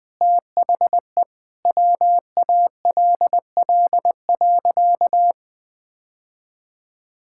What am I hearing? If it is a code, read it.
Morse code, heard as a keyed tone, about 20 wpm, 700 Hz. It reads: THE WALL.
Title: Sentences with 2-Word Phrases - Spoken Beginning - 20wpm